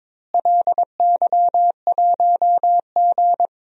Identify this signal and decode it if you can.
Morse code, heard as a keyed tone, about 22 wpm, 700 Hz. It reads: LY1G
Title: DXCC-100 Call Signs - 4 Characters - Repeated 3x - 22wpm